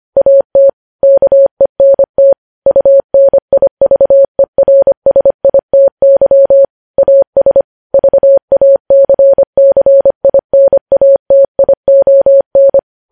Morse code, heard as a keyed tone, about 25 wpm, 550 Hz. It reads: AT KENT UNIVERSITY AS VACCINATION
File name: podcast / News headlines-1x-25x25